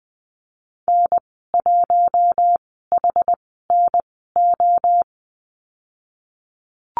Morse code, heard as a keyed tone, about 19 wpm, spaced as for 14 wpm, 700 Hz. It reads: N1HNO
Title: 062 Lesson 11g - Letter h - Call Signs 20wpm